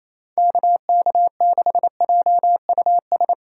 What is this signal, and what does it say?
Morse code, heard as a keyed tone, about 28 wpm, 700 Hz. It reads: KK6JUS
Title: USA Call Signs - 6 Characters - 2x Character Spacing - 28wpm